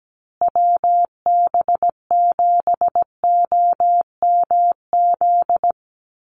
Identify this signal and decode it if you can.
Morse code, heard as a keyed tone, about 17 wpm, 700 Hz. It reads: WB7OMZ
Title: USA Call Signs Repeated 2x 17wpm